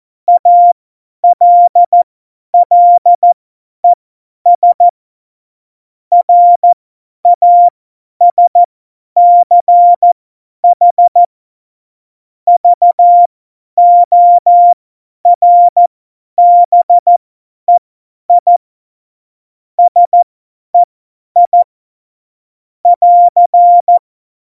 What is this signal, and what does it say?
Morse code, heard as a keyed tone, about 14 wpm, spaced as for 10 wpm, 700 Hz. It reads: ALLES RASCH VORBEI SEI <AR>
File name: Morsekurs / Witze / Witz 17. November 2020, 50 F70